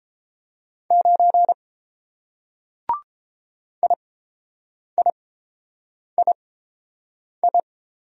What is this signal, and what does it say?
Morse code, heard as a keyed tone, about 29 wpm, spaced as for 7 wpm, 700 Hz. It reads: 9 IIII